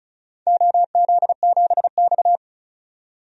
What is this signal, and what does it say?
Morse code, heard as a keyed tone, about 36 wpm, 700 Hz. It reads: OZ7X
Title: DXCC-100 Call Signs - 4 Characters - 35wpm